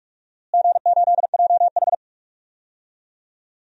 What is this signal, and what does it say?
Morse code, heard as a keyed tone, about 46 wpm, 700 Hz. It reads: G8JH
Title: DXCC-100 Call Signs - 4 Characters - Repeated 3x - 45wpm